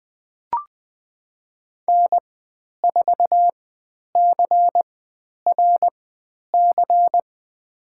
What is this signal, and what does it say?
Morse code, heard as a keyed tone, about 20 wpm, spaced as for 10 wpm, 700 Hz. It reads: N4CRC
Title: 102 Lesson 17g - Letter u - Call Signs 20wpm